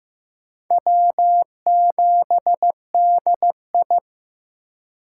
Morse code, heard as a keyed tone, about 15 wpm, 700 Hz. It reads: W7DI
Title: USA Call Signs - 4 Characters - 2x Character Spacing - 15wpm